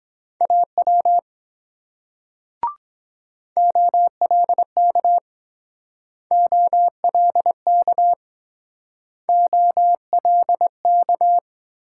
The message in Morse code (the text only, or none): AW OLK OLK OLK